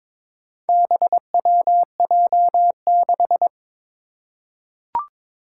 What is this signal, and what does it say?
Morse code, heard as a keyed tone, about 22 wpm, 700 Hz. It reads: BWJ6
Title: Four Challenging Letter-Number Combinations 22wpm